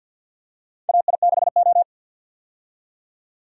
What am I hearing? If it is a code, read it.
Morse code, heard as a keyed tone, about 53 wpm, 700 Hz. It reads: AI6O
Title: USA Call Signs - 4 Characters - 50wpm